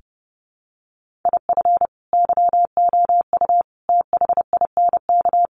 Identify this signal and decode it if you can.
Morse code, heard as a keyed tone, about 30 wpm, 700 Hz. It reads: IF YOU THINK
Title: Sets of 5 Words Encore2 30wpm